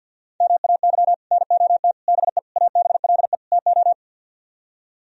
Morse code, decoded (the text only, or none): MAY NOT BE ABLE TO